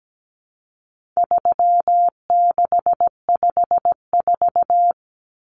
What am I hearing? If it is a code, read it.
Morse code, heard as a keyed tone, about 17 wpm, 700 Hz. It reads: 3654
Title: Common Cut Numbers - 4-Digit 17wpm